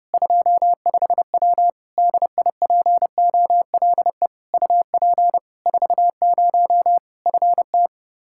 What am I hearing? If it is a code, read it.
Morse code, heard as a keyed tone, about 30 wpm, 700 Hz. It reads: 25W DIPOLE UP 40 FT
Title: K1USN SST QSO Practice 30wpm